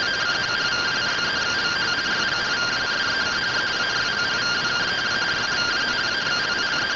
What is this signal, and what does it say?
Two-tone teleprinter signal, about 170 Hz shift.